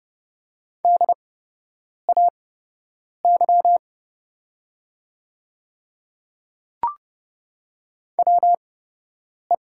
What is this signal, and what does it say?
Morse code, heard as a keyed tone, about 30 wpm, spaced as for 8 wpm, 690 Hz. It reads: DAY WE
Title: 8x Character Spacing - Top 200 Words - 30wpm